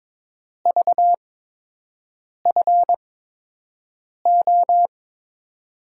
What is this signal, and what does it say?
Morse code, heard as a keyed tone, about 21 wpm, spaced as for 6 wpm, 700 Hz. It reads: VFO
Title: USA Call Signs - 5 Characters - 8x Character Spacing - 22wpm